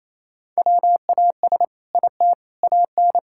Morse code, heard as a keyed tone, about 27 wpm, 700 Hz. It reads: WAS IT AN
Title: Sentences from Top 1000 Words - Review Part 6 - 28wpm